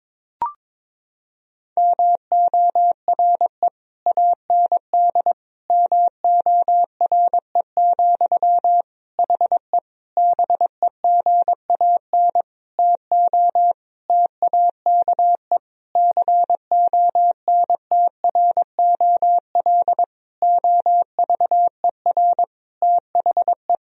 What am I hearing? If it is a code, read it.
Morse code, heard as a keyed tone, about 22 wpm, 700 Hz. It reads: MORE AND MORE, HE BEGAN TO TAKE CONTROL OVER THE
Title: Sentences from Top 1000 Words - Review Part 3 - 22wpm